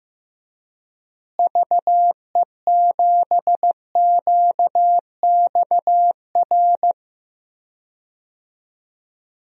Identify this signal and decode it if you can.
Morse code, heard as a keyed tone, about 15 wpm, 700 Hz. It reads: VE7QXR
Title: DXCC-100 Call Signs - 6 Characters - Repeated 2x - 15wpm